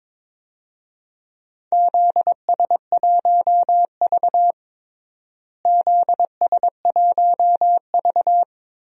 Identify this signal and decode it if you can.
Morse code, heard as a keyed tone, about 22 wpm, 700 Hz. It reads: ZS1V ZS1V
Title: DXCC-100 Call Signs - 4 Characters - Repeated 3x - 22wpm